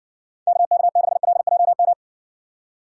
Morse code, heard as a keyed tone, about 60 wpm, 700 Hz. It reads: KK6LJM